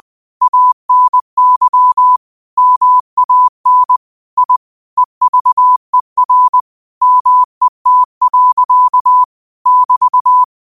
Morse code, heard as a keyed tone, about 20 wpm, 1000 Hz. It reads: ANY MAN I EVER MET. <BT>